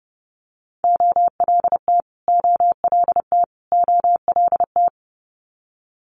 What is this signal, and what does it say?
Morse code, heard as a keyed tone, about 30 wpm, 700 Hz. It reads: OLT OLT OLT
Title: Multi-letter Phonemes Repeated 3x 30wpm